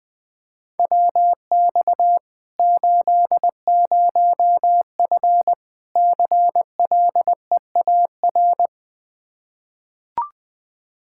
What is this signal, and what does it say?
Morse code, heard as a keyed tone, about 20 wpm, 700 Hz. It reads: WX 80F CLEAR
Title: Practice QSO - Exchange 2 - Info - 20wpm